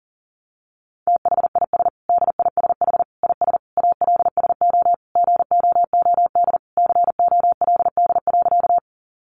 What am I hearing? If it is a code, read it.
Morse code, heard as a keyed tone, about 40 wpm, 700 Hz. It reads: THIS DISH IS ALSO GOOD COLD.
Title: Sentences from Top 1300 Words 40wpm